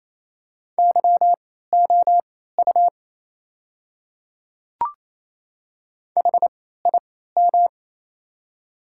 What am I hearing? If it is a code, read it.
Morse code, heard as a keyed tone, about 28 wpm, spaced as for 16 wpm, 700 Hz. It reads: YOU HIM
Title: 3x Character Spacing - Top 100 Words - 28wpm